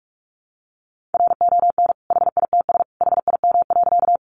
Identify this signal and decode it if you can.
Morse code, heard as a keyed tone, about 44 wpm, 700 Hz. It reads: RON HITS HIM.
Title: Additional Easy Phrases 45wpm